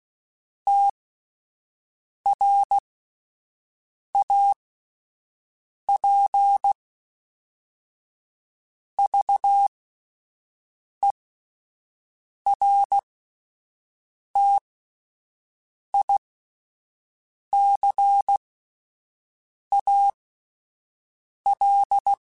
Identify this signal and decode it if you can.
Morse code, heard as a keyed tone, about 16 wpm, spaced as for 5 wpm, 800 Hz. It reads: TRAP VERTICAL
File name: fromSMs / FromSMs mp3s / Antennas@5